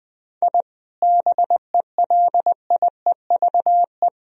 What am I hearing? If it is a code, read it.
Morse code, heard as a keyed tone, about 20 wpm, 700 Hz. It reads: I BELIEVE
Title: Sentences from Top 1000 Words - Review Part 3 - 20wpm